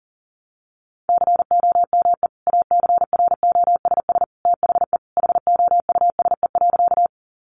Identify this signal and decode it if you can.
Morse code, heard as a keyed tone, about 40 wpm, 700 Hz. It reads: COME ACROSS THE HOUSE.
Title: Sentences from Top 700 Words 40wpm